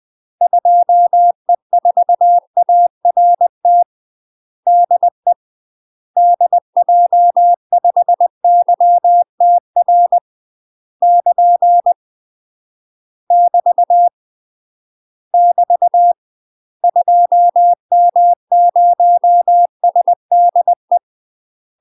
Morse code, heard as a keyed tone, about 20 wpm, 700 Hz. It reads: 2E4ART DE DJ5YTR <KN> <BT> <BT> 2M0SDE
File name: UK MASTER QSOs TRAINING / 20WPM / UK MASTER QSOs 0002